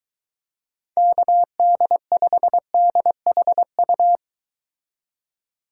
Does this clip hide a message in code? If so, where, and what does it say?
Yes: Morse code from 1.0 to 4.2 s, heard as a keyed tone, about 22 wpm, 700 Hz. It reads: KD5DHU